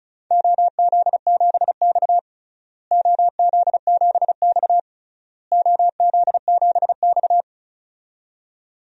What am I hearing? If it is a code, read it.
Morse code, heard as a keyed tone, about 36 wpm, 700 Hz. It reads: OZ7X OZ7X OZ7X
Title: DXCC-100 Call Signs - 4 Characters - Repeated 3x - 35wpm